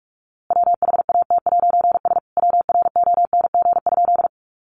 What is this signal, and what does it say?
Morse code, heard as a keyed tone, about 45 wpm, 700 Hz. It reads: WHAT'S WRONG?
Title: A Sound Of Thunder 45wpm